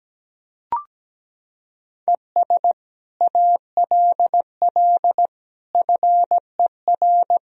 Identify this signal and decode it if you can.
Morse code, heard as a keyed tone, about 17 wpm, 700 Hz. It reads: ES ALL FER
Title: Sentences from Top 100 Words and Abbreviations 17wpm